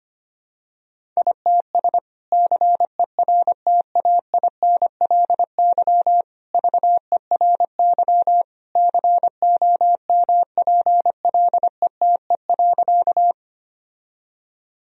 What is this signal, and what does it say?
Morse code, heard as a keyed tone, about 25 wpm, 700 Hz. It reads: ITS CERTAINLY VERY COMPLETE.
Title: Sentences from Top 1000 Words - Review Part 2 - 25wpm